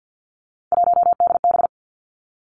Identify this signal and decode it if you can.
Morse code, heard as a keyed tone, about 50 wpm, 700 Hz. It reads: 1DB
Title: Three Challenging Letter-Number Combinations 50wpm